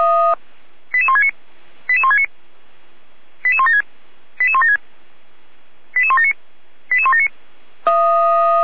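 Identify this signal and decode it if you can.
Five-tone selective call (ZVEI), heard as quick runs of five tones, tones of about 70 ms. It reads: 88169 88169 88176 88176 88179 88179
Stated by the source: ZVEI five-tone call received via SDR.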